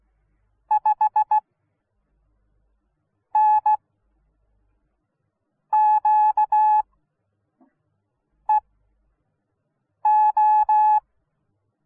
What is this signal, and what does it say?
Morse code, heard as a keyed tone, about 16 wpm, spaced as for 4 wpm, 850 Hz. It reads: 5NQEO